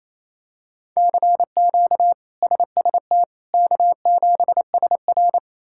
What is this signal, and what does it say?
Morse code, heard as a keyed tone, about 28 wpm, 700 Hz. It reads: CQ SST K7SR
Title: K1USN SST QSO Practice - 28wpm